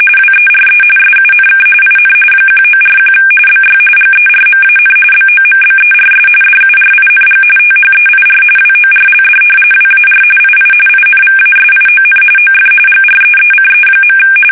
RTTY, about 45.45 baud. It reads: WERE SUNK IN 1941 BY GERMAN BOMBERS DURING THEIR WORLD WAR II INVASION OF GREECE.
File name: fsk850 / battleships